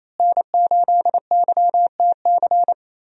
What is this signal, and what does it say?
Morse code, heard as a keyed tone, about 28 wpm, 700 Hz. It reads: N8YTC
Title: USA Call Signs - 5 Characters - Repeated 2x - 28wpm